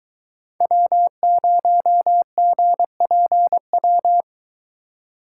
Morse code, heard as a keyed tone, about 23 wpm, 700 Hz. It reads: W0GPW